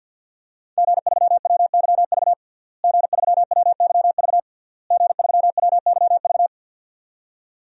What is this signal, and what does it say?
Morse code, heard as a keyed tone, about 49 wpm, 700 Hz. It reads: G3WYV G3WYV G3WYV
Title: DXCC-100 Call Signs - 5 Characters - Repeated 3x - 50wpm